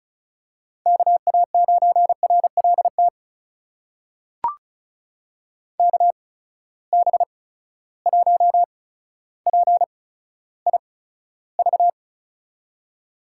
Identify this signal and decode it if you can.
Morse code, heard as a keyed tone, about 36 wpm, spaced as for 15 wpm, 700 Hz. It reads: KA9RLT KB1PIV
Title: USA Call Signs - 6 Characters - 8x Character Spacing - 35wpm